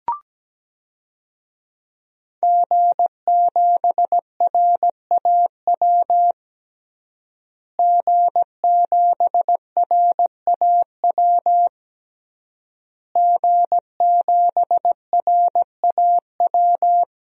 Morse code, heard as a keyed tone, about 17 wpm, 700 Hz. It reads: G7RAW G7RAW G7RAW
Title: DXCC-100 Call Signs - 5 Characters - Repeated 3x - 17wpm